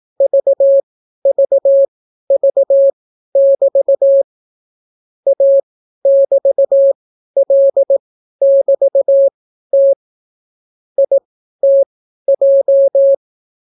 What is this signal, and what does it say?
Morse code, heard as a keyed tone, about 18 wpm, spaced as for 12 wpm, 550 Hz. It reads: VVV<BT> A<BT>L<BT>T ITJ